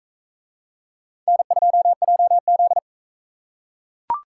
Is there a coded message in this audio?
Yes: Morse code from 1.3 to 2.8 s, heard as a keyed tone, about 40 wpm, 700 Hz. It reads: N2JZ